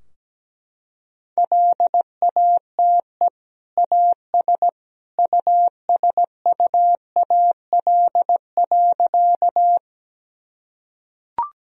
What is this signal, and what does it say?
Morse code, heard as a keyed tone, about 17 wpm, 700 Hz. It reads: LATE AS USUAL.
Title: Sentences from Top 1000 Words - Review Part 2 - 17wpm